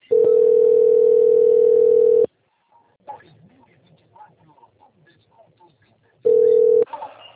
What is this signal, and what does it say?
Ringback tone (North America), 440 + 480 Hz.
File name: out- - - - - .16833